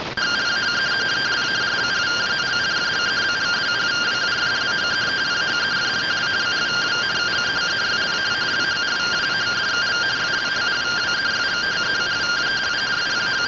Two-tone teleprinter signal, about 170 Hz shift.